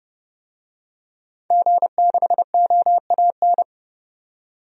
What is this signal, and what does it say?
Morse code, heard as a keyed tone, about 30 wpm, 700 Hz. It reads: G6OAN
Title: DXCC-100 Call Signs - 5 Characters - 30wpm